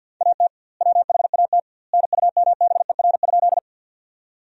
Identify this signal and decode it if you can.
Morse code, heard as a keyed tone, about 50 wpm, 700 Hz. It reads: AT WHAT NUMBER?
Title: 2x Word-Spacing - Sentences from Top 100 Words - 50wpm